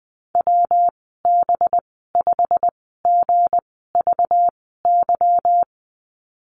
Morse code, heard as a keyed tone, about 20 wpm, spaced as for 15 wpm, 700 Hz. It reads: WB5GVY
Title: 193 Lesson 31g - Letter v - Call Signs 20wpm